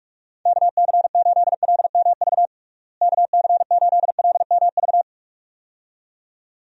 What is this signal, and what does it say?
Morse code, heard as a keyed tone, about 46 wpm, 700 Hz. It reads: KC8LMV KC8LMV
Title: USA Call Signs - 6 Characters - Repeated 2x - 45wpm